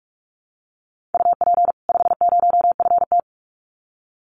Morse code, heard as a keyed tone, about 44 wpm, 700 Hz. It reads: UP 50FT
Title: Common QSO Phrases 45wpm